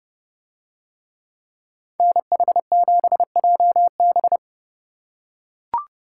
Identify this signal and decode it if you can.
Morse code, heard as a keyed tone, about 29 wpm, 700 Hz. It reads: NH7JB